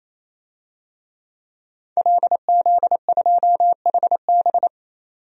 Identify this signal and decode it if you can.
Morse code, heard as a keyed tone, about 27 wpm, 700 Hz. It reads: LZ2HB
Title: DXCC-100 Call Signs - 5 Characters - 28wpm